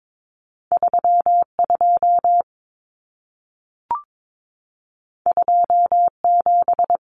Morse code, heard as a keyed tone, about 22 wpm, 700 Hz. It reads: 32 27
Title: Common Cut Numbers - 2-Digit 22wpm